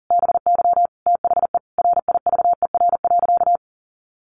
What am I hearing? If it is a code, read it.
Morse code, heard as a keyed tone, about 40 wpm, 700 Hz. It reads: BY THE RIVER.
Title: Sentences from Top 1200 Words 40wpm